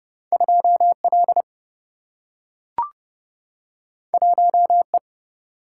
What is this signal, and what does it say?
Morse code, heard as a keyed tone, about 30 wpm, 700 Hz. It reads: 2L 1E
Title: Single-Letter Single-Number 30wpm